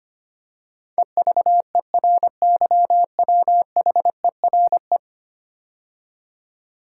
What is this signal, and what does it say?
Morse code, heard as a keyed tone, about 25 wpm, 700 Hz. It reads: EVERYWHERE
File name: 10 Letter Words - First 3 Letters Given - 25wpm